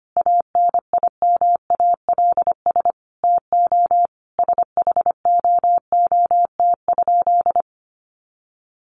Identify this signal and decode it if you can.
Morse code, heard as a keyed tone, about 25 wpm, 700 Hz. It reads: ANIMALS TO SHOOT?
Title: A Sound Of Thunder 25wpm